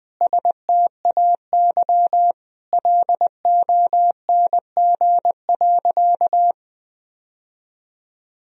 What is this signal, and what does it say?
Morse code, heard as a keyed tone, about 21 wpm, 700 Hz. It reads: STAY LONG.
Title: Sentences from Top 1000 Words - Review Part 3 - 20wpm